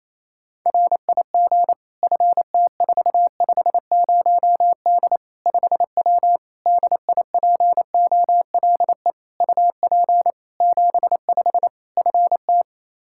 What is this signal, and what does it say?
Morse code, heard as a keyed tone, about 28 wpm, 700 Hz. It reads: RIG FT450D 5W DIPOLE UP 75 FT
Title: Practice QSO - Exchange 2 - Info - 28wpm